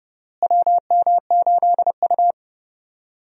Morse code, heard as a keyed tone, about 31 wpm, 700 Hz. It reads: WM8U
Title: USA Call Signs - 4 Characters - Repeated 3x - 30wpm